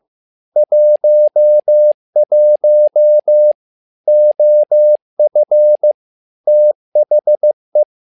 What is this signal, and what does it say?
Morse code, heard as a keyed tone, about 15 wpm, 600 Hz. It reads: 11 OF THE
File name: learning center / tower trouble / tower trouble 15wpm